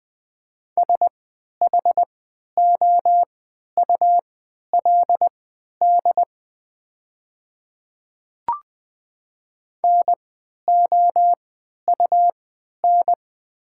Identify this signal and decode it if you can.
Morse code, heard as a keyed tone, about 20 wpm, spaced as for 11 wpm, 700 Hz. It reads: SHOULD NOUN
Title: 3x Character Spacing - Top 500 Words - 20wpm